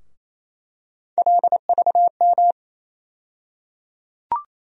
Morse code, heard as a keyed tone, about 28 wpm, 700 Hz. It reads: LVM
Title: USA Call Sign Postfixes - 3 Letters 28wpm